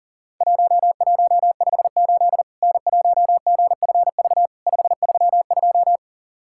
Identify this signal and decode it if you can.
Morse code, heard as a keyed tone, about 40 wpm, 700 Hz. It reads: 1158 N1GFV 532
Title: Worked All Europe DX Contest - QTC Exchange - 40wpm